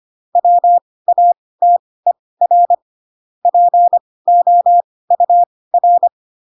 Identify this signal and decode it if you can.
Morse code, heard as a keyed tone, about 25 wpm, spaced as for 18 wpm, 700 Hz. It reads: WATER POUR
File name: Pinocchio / Pinocchio ch6 12 25 ik1hge